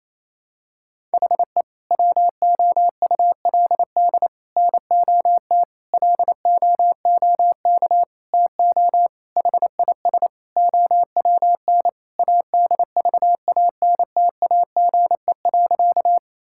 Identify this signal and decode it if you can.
Morse code, heard as a keyed tone, about 28 wpm, 700 Hz. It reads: HE WOULD NOT LOOK TO HIS OWN ADVANTAGE.
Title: Sentences from Top 1000 Words - Review Part 6 - 28wpm